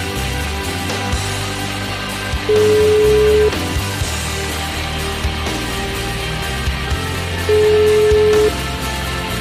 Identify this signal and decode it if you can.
Ringback tone (Europe, 425 Hz), with noise.